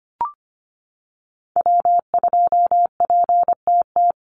Morse code, heard as a keyed tone, about 25 wpm, 700 Hz. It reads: W2PTT
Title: Call Signs - 25wpm